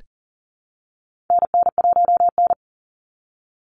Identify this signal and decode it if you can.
Morse code, heard as a keyed tone, about 40 wpm, 700 Hz. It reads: NN1N
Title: Call Signs Repeated 3x - 40wpm